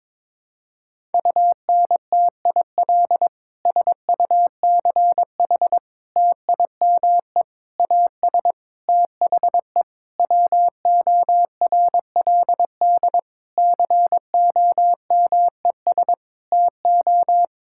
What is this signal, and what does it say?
Morse code, heard as a keyed tone, about 22 wpm, 700 Hz. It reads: UNTIL SUCH TIME AS THE WORLD COMES TO
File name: Sentences with 9-Word Phrases - 22wpm